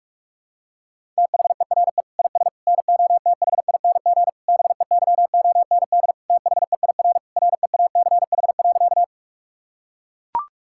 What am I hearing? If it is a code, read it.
Morse code, heard as a keyed tone, about 42 wpm, 700 Hz. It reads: THERE IS NOTHING BEYOND THEIR REACH.
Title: Sentences from Top 1000 Words - Review Part 6 - 45wpm